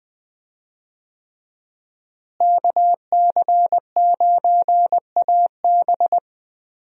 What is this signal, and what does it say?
Morse code, heard as a keyed tone, about 20 wpm, 700 Hz. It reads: KC9AB
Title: USA Call Signs - 5 Characters - Repeated 3x - 20wpm